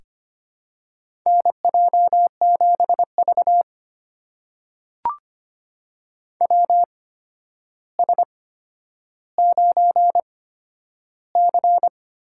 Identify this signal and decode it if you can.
Morse code, heard as a keyed tone, about 25 wpm, spaced as for 11 wpm, 700 Hz. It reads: NJ7V WS9C
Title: USA Call Signs - 4 Characters - 8x Character Spacing - 25wpm